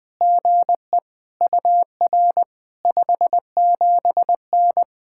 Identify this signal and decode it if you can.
Morse code, heard as a keyed tone, about 20 wpm, 700 Hz. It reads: GE UR 57N
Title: POTA QSO Practice - 20wpm